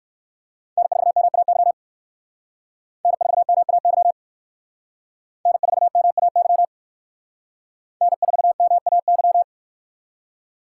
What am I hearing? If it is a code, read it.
Morse code, heard as a keyed tone, about 57 wpm, 700 Hz. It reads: N4MAY N4MAY N4MAY N4MAY